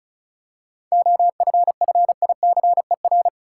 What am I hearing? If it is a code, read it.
Morse code, heard as a keyed tone, about 34 wpm, 700 Hz. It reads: OFFICER
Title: Seven Letter Words - 35wpm